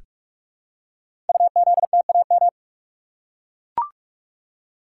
Morse code, heard as a keyed tone, about 44 wpm, 700 Hz. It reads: UZTAM
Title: Code Groups - 5 Letters - 45wpm